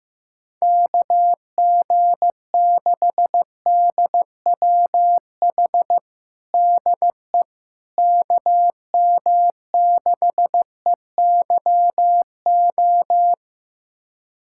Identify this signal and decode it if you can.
Morse code, heard as a keyed tone, about 15 wpm, 700 Hz. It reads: KG6DWH DE KM6EYO
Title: USA Callsign DE Callsign - 6 Characters - 15wpm